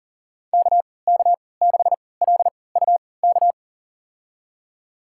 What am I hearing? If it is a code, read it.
Morse code, heard as a keyed tone, about 40 wpm, spaced as for 23 wpm, 700 Hz. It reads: KK6LUK
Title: USA Call Signs - 6 Characters - 3x Character Spacing - 40wpm